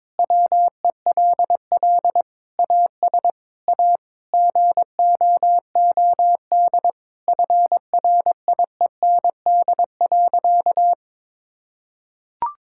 Morse code, heard as a keyed tone, about 22 wpm, 700 Hz. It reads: WELL AS A GOOD FRIEND.
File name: practice / 22wpm / Sentences with 4-Word Phrases - 22wpm